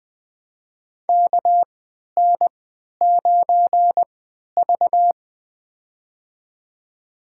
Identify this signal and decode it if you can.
Morse code, heard as a keyed tone, about 20 wpm, spaced as for 11 wpm, 700 Hz. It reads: KN9V
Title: USA Call Signs - 4 Characters - 3x Character Spacing - 20wpm